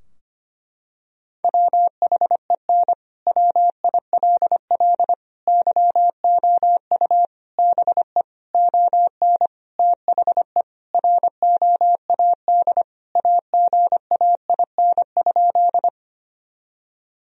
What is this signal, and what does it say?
Morse code, heard as a keyed tone, about 25 wpm, 700 Hz. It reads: WHEN WILL YOU BE ON THE ROAD AGAIN?
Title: Sentences from Top 1000 Words - Review Part 2 - 25wpm